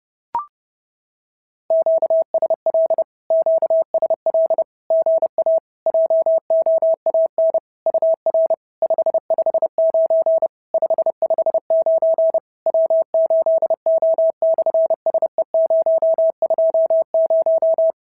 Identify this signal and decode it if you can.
Morse code, heard as a keyed tone, about 30 wpm, 650 Hz. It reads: QSL QSL GA JOAN UR 559 559 W8O/SE020
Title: SOTA QSO Practice 30wpm